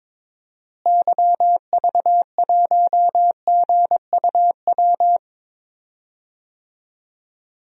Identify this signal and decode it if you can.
Morse code, heard as a keyed tone, about 22 wpm, 700 Hz. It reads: YV1GUW
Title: DXCC-100 Call Signs - 6 Characters - Repeated 3x - 22wpm